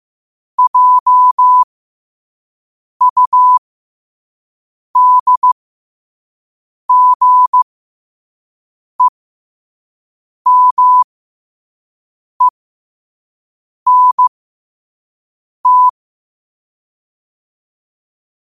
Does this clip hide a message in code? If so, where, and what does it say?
Yes: Morse code from 0.6 to 15.9 s, heard as a keyed tone, about 15 wpm, spaced as for 5 wpm, 1000 Hz. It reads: JUDGEMENT